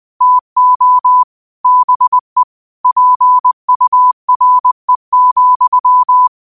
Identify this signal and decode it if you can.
Morse code, heard as a keyed tone, about 20 wpm, 1000 Hz. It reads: TO BE PURE,
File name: qotdfiles / 20wpm